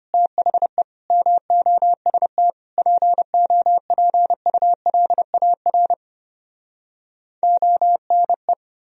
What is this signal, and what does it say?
Morse code, heard as a keyed tone, about 33 wpm, 700 Hz. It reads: THE MOST POPULAR ONE